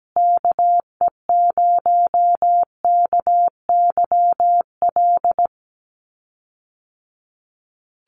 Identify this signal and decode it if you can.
Morse code, heard as a keyed tone, about 17 wpm, 700 Hz. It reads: KE0KYL
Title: USA Call Signs Repeated 3x 17wpm